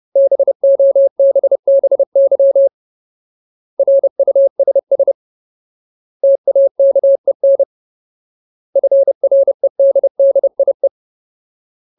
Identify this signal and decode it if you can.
Morse code, heard as a keyed tone, about 30 wpm, 550 Hz. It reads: BOBBY RUSS TAKEN FREDDIE
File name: Ultimate Word List / 30WPM Ultimate